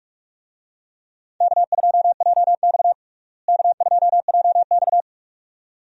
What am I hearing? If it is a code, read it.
Morse code, heard as a keyed tone, about 42 wpm, 700 Hz. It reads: K2JX K2JX
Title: USA Call Signs - 4 Characters - Repeated 2x - 45wpm